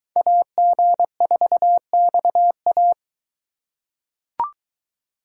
Morse code, heard as a keyed tone, about 21 wpm, 700 Hz. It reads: AG4XA